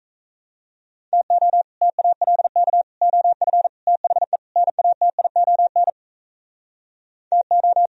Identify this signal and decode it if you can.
Morse code, heard as a keyed tone, about 44 wpm, 700 Hz. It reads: TO TALK OF THE NATION TO